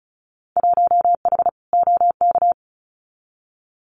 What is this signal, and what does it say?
Morse code, heard as a keyed tone, about 34 wpm, 700 Hz. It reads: 1H OK
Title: Winter Field Day Exchanges 35wpm